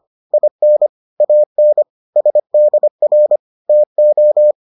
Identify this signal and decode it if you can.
Morse code, heard as a keyed tone, about 25 wpm, 600 Hz. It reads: IN AN SDR TO